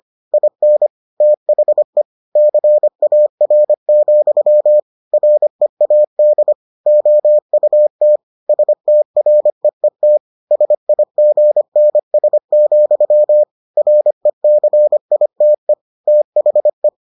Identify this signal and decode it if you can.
Morse code, heard as a keyed tone, about 25 wpm, 600 Hz. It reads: IN THE CAR, READ OUT STREET SIGNS, RECITE THE